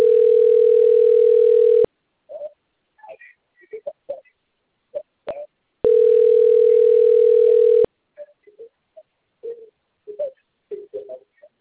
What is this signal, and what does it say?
Ringback tone (North America), 440 + 480 Hz.
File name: out- - - - - .63087